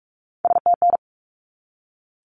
Morse code, heard as a keyed tone, about 44 wpm, 700 Hz. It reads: STN
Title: Abbreviations 45wpm